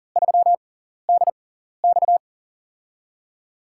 Morse code, heard as a keyed tone, about 40 wpm, spaced as for 14 wpm, 700 Hz. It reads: 3DX
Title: USA Call Signs - 5 Characters - 6x Character Spacing - 40wpm